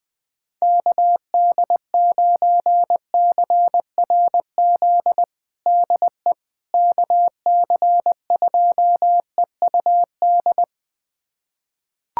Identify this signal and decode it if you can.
Morse code, heard as a keyed tone, about 20 wpm, 700 Hz. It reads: KD9CRZ DE KC2EUD
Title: USA Callsign DE Callsign 20wpm